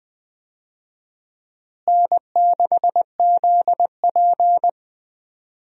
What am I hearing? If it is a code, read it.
Morse code, heard as a keyed tone, about 21 wpm, 700 Hz. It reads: N6ZP
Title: USA Call Signs - 4 Characters - Repeated 3x - 20wpm